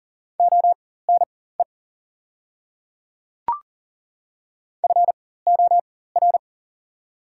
Morse code, heard as a keyed tone, about 42 wpm, spaced as for 19 wpm, 700 Hz. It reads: ONE FOR
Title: 4x Character Spacing - Top 100 Words - 40wpm